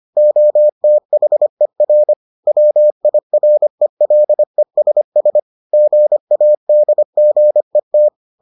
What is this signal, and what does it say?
Morse code, heard as a keyed tone, about 25 wpm, 600 Hz. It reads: OTHER WIRELESS GADGET